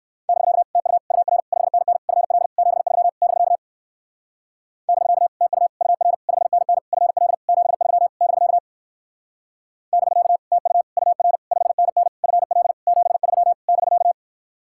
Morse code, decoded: <BK> TU UR 5NN FL 73 <BK> <BK> TU UR 5NN FL 73 <BK> <BK> TU UR 5NN FL 73 <BK>

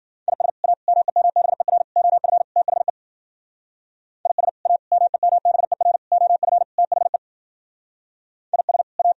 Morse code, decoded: IS A MEMBER OF THE IS A MEMBER OF THE IS A

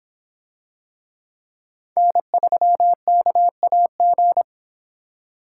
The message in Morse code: N3KAG

